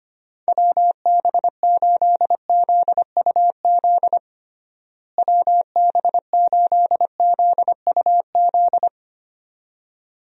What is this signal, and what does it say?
Morse code, heard as a keyed tone, about 25 wpm, 700 Hz. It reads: WB8ZUZ WB8ZUZ
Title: USA Call Signs - 6 Characters - Repeated 2x - 25wpm